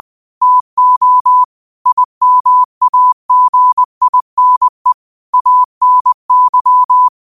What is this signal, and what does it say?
Morse code, heard as a keyed tone, about 20 wpm, 1000 Hz. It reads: TO IMAGINE ANY
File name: qotdfiles / 20wpm